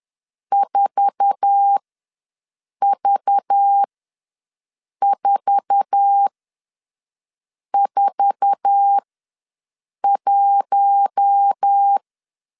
Morse code, heard as a keyed tone, about 10 wpm, spaced as for 5 wpm, 800 Hz. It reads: 4V441
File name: LEZIONI / 8 YJ45 / 8 YJ4 ws 10wpm 4020